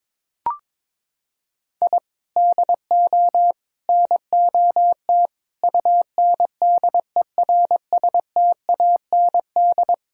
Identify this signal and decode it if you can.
Morse code, heard as a keyed tone, about 22 wpm, 700 Hz. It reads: I DO NOT UNDERSTAND
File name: practice / 22wpm / Sentences with 4-Word Phrases - 22wpm